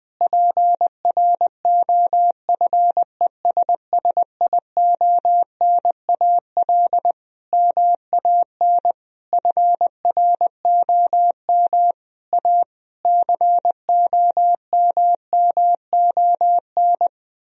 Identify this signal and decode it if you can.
Morse code, heard as a keyed tone, about 21 wpm, 700 Hz. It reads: PROFESSIONAL MAN FROM A COMMON
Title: Sentences from Top 1000 Words - Review Part 4 - 20wpm